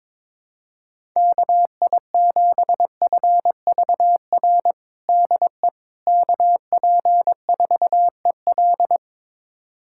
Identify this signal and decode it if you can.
Morse code, heard as a keyed tone, about 22 wpm, 700 Hz. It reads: KI7FVR DE KP4EL
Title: USA Callsign DE Callsign 22wpm